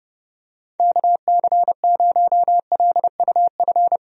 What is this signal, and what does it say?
Morse code, heard as a keyed tone, about 30 wpm, 700 Hz. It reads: KC0LUF
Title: USA Call Signs - 6 Characters - Repeated 2x - 30wpm